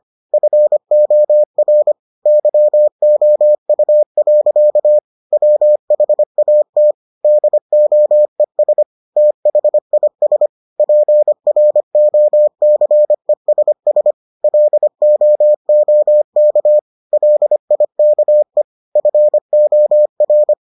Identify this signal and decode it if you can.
Morse code, heard as a keyed tone, about 25 wpm, 600 Hz. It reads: FOR YOU. WHAT DOES THIS PROCESS LOOK LIKE FOR